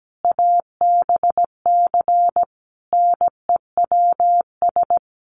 Morse code, heard as a keyed tone, about 17 wpm, 700 Hz. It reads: ABC NEWS
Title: Sets of 5 Words 17wpm